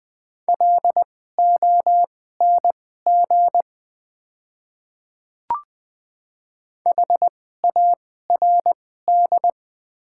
Morse code, heard as a keyed tone, about 20 wpm, spaced as for 14 wpm, 690 Hz. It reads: LONG HARD
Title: 2x Character Spacing - Top 300 Words - 20wpm